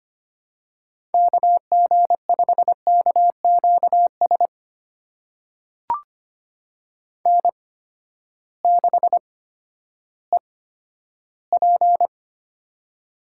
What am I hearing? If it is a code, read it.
Morse code, heard as a keyed tone, about 25 wpm, spaced as for 13 wpm, 700 Hz. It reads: KG5KQS N6EP
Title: USA Call Signs - 8x Character Spacing - 25wpm